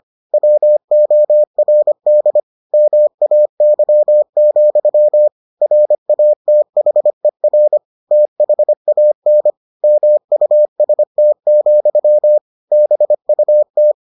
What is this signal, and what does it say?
Morse code, heard as a keyed tone, about 25 wpm, 600 Hz. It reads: WORD MAY, RATHER THAN MUST, BUT